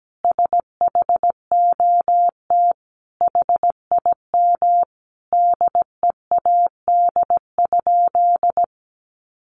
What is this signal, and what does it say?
Morse code, heard as a keyed tone, about 17 wpm, 700 Hz. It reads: SHOT HIM DEAD?
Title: Sentences from Top 1500 Words 17wpm